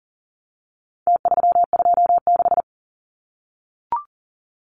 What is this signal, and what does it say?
Morse code, heard as a keyed tone, about 40 wpm, 700 Hz. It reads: T326
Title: Numbers - Contest Style - 4-Digit 40wpm